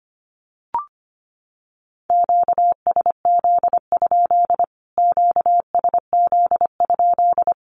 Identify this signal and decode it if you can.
Morse code, heard as a keyed tone, about 25 wpm, 700 Hz. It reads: QSZ? QSZ?
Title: Q-Codes Repeated 3x 25wpm